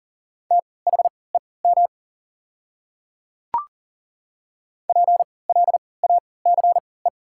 Morse code, heard as a keyed tone, about 40 wpm, spaced as for 23 wpm, 690 Hz. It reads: THEM PLACE
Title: 3x Character Spacing - Top 300 Words - 40wpm